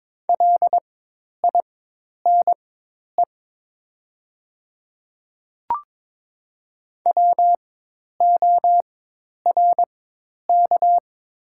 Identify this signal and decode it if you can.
Morse code, heard as a keyed tone, about 22 wpm, spaced as for 10 wpm, 700 Hz. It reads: LINE WORK
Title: 4x Character Spacing - Top 200 Words - 22wpm